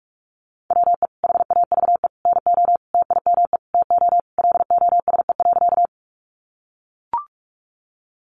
Morse code, heard as a keyed tone, about 44 wpm, 700 Hz. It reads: WE HAVE NO TIME TO LOSE.
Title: Sentences from Top 1100 Words 45wpm